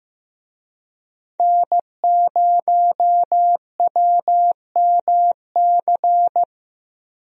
Morse code, heard as a keyed tone, about 15 wpm, 700 Hz. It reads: N0WMC
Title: USA Call Signs - 5 Characters - 15wpm